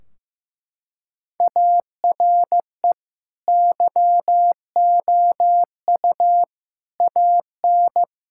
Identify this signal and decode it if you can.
Morse code, heard as a keyed tone, about 15 wpm, 700 Hz. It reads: ARE YOU AN
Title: Sentences from Top 1600 Words 15wpm